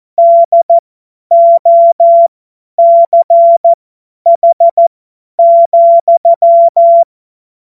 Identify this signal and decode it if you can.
Morse code, heard as a keyed tone, about 14 wpm, spaced as for 10 wpm, 680 Hz. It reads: DOCH,